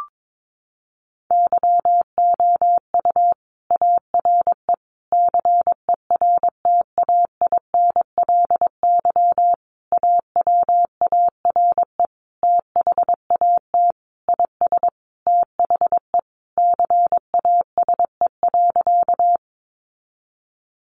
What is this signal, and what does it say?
Morse code, heard as a keyed tone, about 22 wpm, 700 Hz. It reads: YOU ARE CERTAINLY AWARE THAT IS THE CASE.
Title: Sentences from Top 1200 Words 22wpm